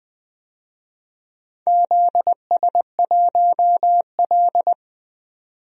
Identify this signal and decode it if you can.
Morse code, heard as a keyed tone, about 20 wpm, 700 Hz. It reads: ZS1L
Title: DXCC-100 Call Signs - 4 Characters - 20wpm